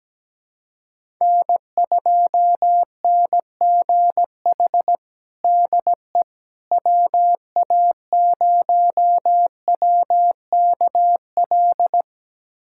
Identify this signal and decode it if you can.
Morse code, heard as a keyed tone, about 17 wpm, 700 Hz. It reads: N2NGH DE WA0WKL
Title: USA Callsign DE Callsign 17wpm